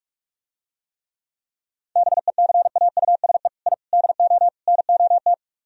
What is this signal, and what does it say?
Morse code, heard as a keyed tone, about 49 wpm, 700 Hz. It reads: BECAUSE I DO NOT